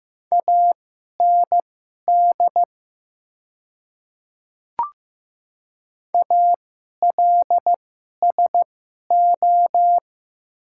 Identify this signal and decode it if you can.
Morse code, heard as a keyed tone, about 15 wpm, spaced as for 11 wpm, 700 Hz. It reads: AND ALSO
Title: 2x Character Spacing - Top 200 Words - 15wpm